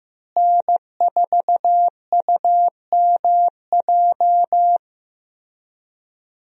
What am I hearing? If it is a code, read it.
Morse code, heard as a keyed tone, about 15 wpm, 700 Hz. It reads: N4UMJ
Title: USA Call Signs - 5 Characters - Repeated 2x - 15wpm